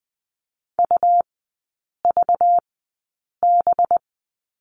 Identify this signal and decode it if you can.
Morse code, heard as a keyed tone, about 20 wpm, spaced as for 8 wpm, 700 Hz. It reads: UVB
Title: 191 Lesson 31e - Letter v - Three Characters 20wpm